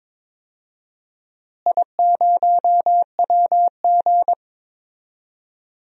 Morse code, heard as a keyed tone, about 22 wpm, 700 Hz. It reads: I0WG
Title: DXCC-100 Call Signs - 4 Characters - Repeated 2x - 22wpm